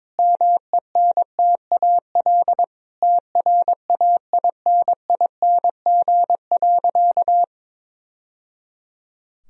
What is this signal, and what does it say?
Morse code, heard as a keyed tone, about 22 wpm, 700 Hz. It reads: MENTAL TRAINING.
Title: Sentences from Top 1000 Words - Review Part 4 - 22wpm